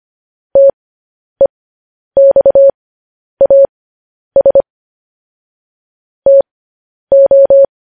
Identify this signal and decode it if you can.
Morse code, heard as a keyed tone, about 25 wpm, spaced as for 10 wpm, 550 Hz. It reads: TEXAS TO